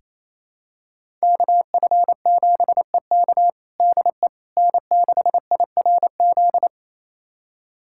Morse code, heard as a keyed tone, about 28 wpm, 700 Hz. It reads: KF7EK DE N6IRZ
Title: USA Callsign DE Callsign - 5 Characters - 28wpm